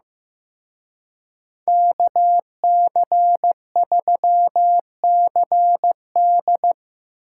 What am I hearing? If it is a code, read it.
Morse code, heard as a keyed tone, about 18 wpm, 700 Hz. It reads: KC3CD
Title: Speed Racing - USA Call Signs 15wpm